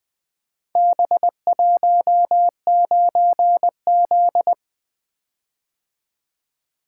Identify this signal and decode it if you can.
Morse code, heard as a keyed tone, about 20 wpm, 700 Hz. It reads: B19Z
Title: Four Letter-Number Combinations - 20wpm